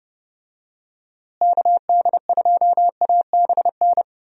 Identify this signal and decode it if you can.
Morse code, heard as a keyed tone, about 31 wpm, 700 Hz. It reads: KD2ABN
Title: USA Call Signs - 6 Characters - Repeated 2x - 30wpm